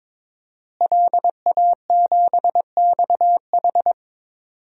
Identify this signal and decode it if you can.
Morse code, heard as a keyed tone, about 22 wpm, 700 Hz. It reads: LA7XH
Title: DXCC-100 Call Signs - 5 Characters - Repeated 3x - 22wpm